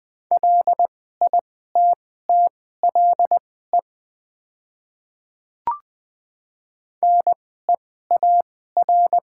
Morse code, heard as a keyed tone, about 20 wpm, spaced as for 14 wpm, 700 Hz. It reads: LITTLE NEAR
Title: 2x Character Spacing - Top 200 Words - 20wpm